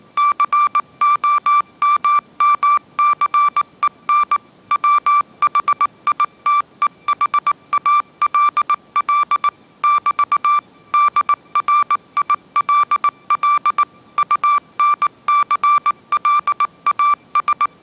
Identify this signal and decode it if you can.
Morse code, heard as a keyed tone, about 19 wpm, 1170 Hz. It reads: COMMCEN WHITEHALL <BT> DRILL UNCLAS
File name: mrx083 20wpm